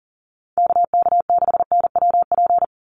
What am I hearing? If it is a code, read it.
Morse code, heard as a keyed tone, about 37 wpm, 700 Hz. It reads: KK6NWP
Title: USA Call Signs 40wpm